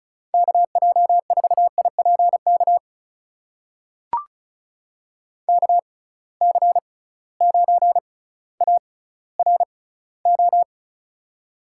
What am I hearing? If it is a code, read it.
Morse code, heard as a keyed tone, about 36 wpm, spaced as for 18 wpm, 700 Hz. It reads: KJ4IPK KC9ARO
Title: USA Call Signs - 6 Characters - 6x Character Spacing - 35wpm